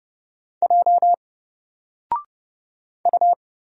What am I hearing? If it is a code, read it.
Morse code, heard as a keyed tone, about 30 wpm, 700 Hz. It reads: J U
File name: Single-Letter-Number-30wpm